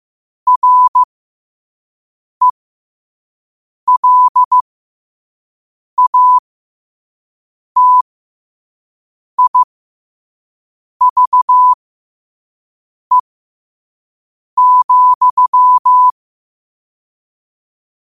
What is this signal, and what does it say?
Morse code, heard as a keyed tone, about 15 wpm, spaced as for 5 wpm, 1000 Hz. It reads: RELATIVE,
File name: qotdfiles / 5wpm